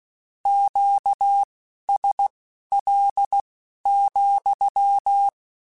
Morse code, heard as a keyed tone, about 16 wpm, spaced as for 11 wpm, 800 Hz. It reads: QSL,
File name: Q Calls / Q Calls-mp3s / Q Calls SPSL2 @15